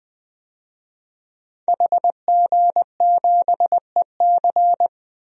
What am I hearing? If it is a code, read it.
Morse code, heard as a keyed tone, about 20 wpm, 700 Hz. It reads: HG7EC
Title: DXCC-100 Call Signs - 5 Characters - Repeated 3x - 20wpm